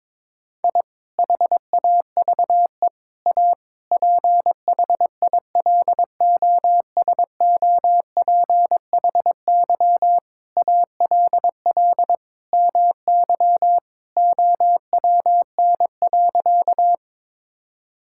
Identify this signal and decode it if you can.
Morse code, heard as a keyed tone, about 22 wpm, 700 Hz. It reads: I HAVE A PHILOSOPHY ALL MY OWN.
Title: Sentences from Top 1000 Words - Review Part 6 - 22wpm